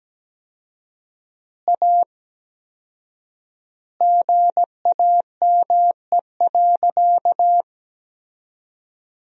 Morse code, heard as a keyed tone, about 17 wpm, 700 Hz. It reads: A GAME.
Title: 4x Word-Spacing - Sentences from Top 500 Words - 17wpm